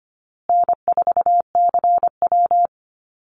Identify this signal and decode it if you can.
Morse code, heard as a keyed tone, about 25 wpm, 700 Hz. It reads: N4CW
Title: Call Signs Repeated 2x - 25wpm